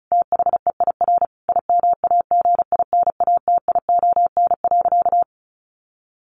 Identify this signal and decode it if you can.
Morse code, heard as a keyed tone, about 34 wpm, 700 Hz. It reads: THEIR IMAGINATION.
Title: Sentences from Top 1300 Words 35wpm